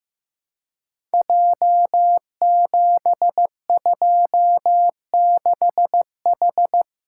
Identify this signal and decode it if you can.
Morse code, heard as a keyed tone, about 15 wpm, 700 Hz. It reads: J726H
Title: Code Groups - 5 Letter:Numbers - 15wpm